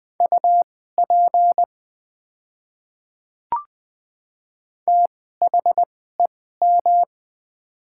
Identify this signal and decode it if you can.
Morse code, heard as a keyed tone, about 20 wpm, spaced as for 15 wpm, 700 Hz. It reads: UP THEM
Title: 2x Character Spacing - Top 200 Words - 20wpm